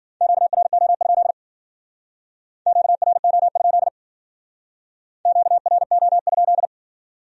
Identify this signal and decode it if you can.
Morse code, heard as a keyed tone, about 51 wpm, 700 Hz. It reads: QRO? QRO? QRO?